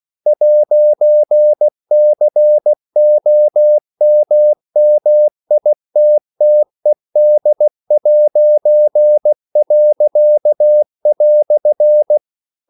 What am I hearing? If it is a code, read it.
Morse code, heard as a keyed tone, about 16 wpm, 600 Hz. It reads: 'COMMITTED'."
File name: CW / Practice / EnglishQuotations0116